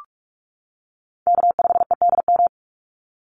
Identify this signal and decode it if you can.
Morse code, heard as a keyed tone, about 44 wpm, 700 Hz. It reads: K5EDM
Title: Call Signs - 45wpm